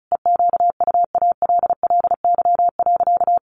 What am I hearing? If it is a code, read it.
Morse code, heard as a keyed tone, about 34 wpm, 700 Hz. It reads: EQUALLY.
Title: Sentences from Top 1400 Words 35wpm